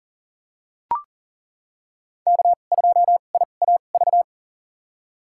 Morse code, heard as a keyed tone, about 39 wpm, spaced as for 29 wpm, 700 Hz. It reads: K2IAV
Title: USA Call Signs - 5 Characters - 2x Character Spacing - 40wpm